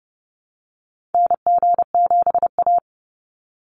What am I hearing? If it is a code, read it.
Morse code, heard as a keyed tone, about 30 wpm, 700 Hz. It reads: NG7A
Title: Call Signs Repeated 3x - 30wpm